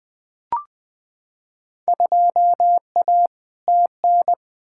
Morse code, heard as a keyed tone, about 20 wpm, 700 Hz. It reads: 2A TN
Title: Field Day Exchanges Encore - 20wpm